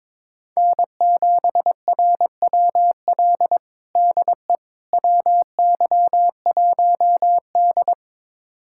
Morse code, heard as a keyed tone, about 22 wpm, 700 Hz. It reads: N7RWL DE WY1D
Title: USA Callsign DE Callsign 22wpm